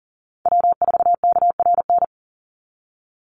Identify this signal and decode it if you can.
Morse code, heard as a keyed tone, about 38 wpm, 700 Hz. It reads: W4KRN
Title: NA SOTA Chaser Callsigns 40wpm